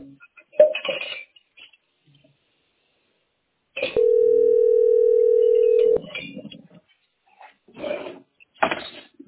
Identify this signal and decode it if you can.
Ringback tone (North America), 440 + 480 Hz.